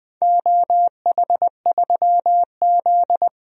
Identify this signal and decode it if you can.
Morse code, heard as a keyed tone, about 21 wpm, 700 Hz. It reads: OH3Z
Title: DXCC-100 Call Signs - 4 Characters - Repeated 2x - 20wpm